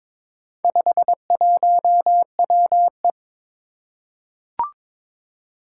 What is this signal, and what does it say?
Morse code, heard as a keyed tone, about 22 wpm, 700 Hz. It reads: 51WE
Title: Four Challenging Letter-Number Combinations 22wpm